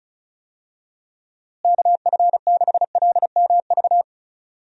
Morse code, heard as a keyed tone, about 34 wpm, 700 Hz. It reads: KF6LMV
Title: USA Call Signs - 6 Characters - Repeated 3x - 35wpm